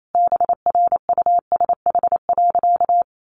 Morse code, heard as a keyed tone, about 28 wpm, 700 Hz. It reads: BRUSH.
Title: Sentences from Top 1100 Words 28wpm